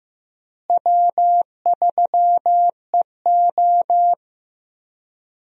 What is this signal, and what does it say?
Morse code, heard as a keyed tone, about 16 wpm, 700 Hz. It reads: W3EO
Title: USA Call Signs - 4 Characters - Repeated 3x - 15wpm